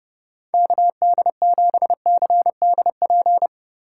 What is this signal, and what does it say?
Morse code, heard as a keyed tone, about 29 wpm, 700 Hz. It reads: KD7CDP